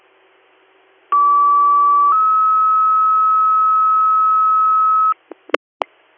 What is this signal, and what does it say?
Two-tone page, heard as a one-second tone, then a three-second tone, about 1150 then 1290 Hz. It is heard from 1.1 to 5.1 s.
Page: allegany-ny